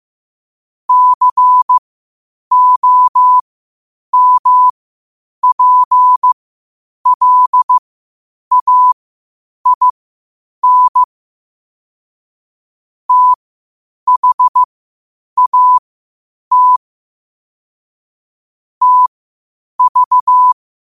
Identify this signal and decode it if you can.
Morse code, heard as a keyed tone, about 15 wpm, spaced as for 8 wpm, 1000 Hz. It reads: COMPLAIN THAT TV
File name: qotdfiles / 8wpm